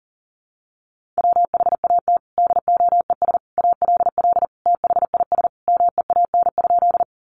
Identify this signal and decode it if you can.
Morse code, heard as a keyed tone, about 40 wpm, 700 Hz. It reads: WHAT DOES ALL THIS MEAN?
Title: Sentences from Top 200 Words 40wpm